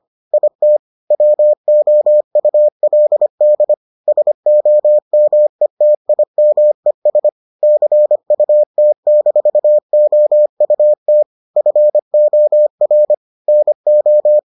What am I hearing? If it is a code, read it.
Morse code, heard as a keyed tone, about 25 wpm, 600 Hz. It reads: IT WOULD SOMETIMES CUT-OUT FOR NO